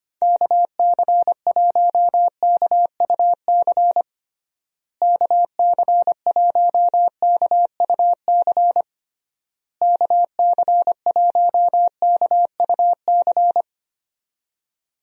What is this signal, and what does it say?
Morse code, heard as a keyed tone, about 25 wpm, 700 Hz. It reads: KC1KUC KC1KUC KC1KUC
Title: USA Call Signs - 6 Characters - Repeated 3x - 25wpm